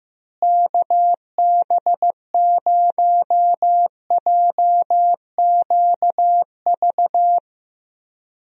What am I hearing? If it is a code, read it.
Morse code, heard as a keyed tone, about 15 wpm, 700 Hz. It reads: KB0JQV
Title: USA Call Signs - 6 Characters - Repeated 3x - 15wpm